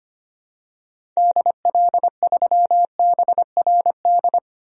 Morse code, heard as a keyed tone, about 25 wpm, 700 Hz. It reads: DL3BRD
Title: DXCC-100 Call Signs - 6 Characters - 25wpm